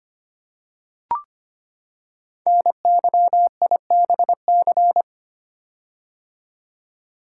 Morse code, heard as a keyed tone, about 25 wpm, 700 Hz. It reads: NYIBC
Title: Code Groups - 5 Letter:Numbers - 25wpm